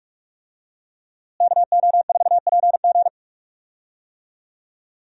Morse code, heard as a keyed tone, about 46 wpm, 700 Hz. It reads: KO4PG
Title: USA Call Signs - 5 Characters - Repeated 3x - 45wpm